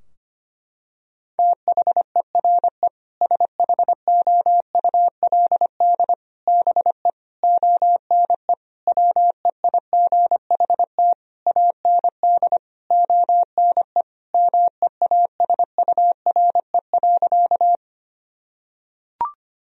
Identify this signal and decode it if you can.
Morse code, heard as a keyed tone, about 25 wpm, 700 Hz. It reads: THERE SHOULD BE ONE WEIGHT AND ONE MEASURE.
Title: Sentences from Top 1000 Words - Review Part 3 - 25wpm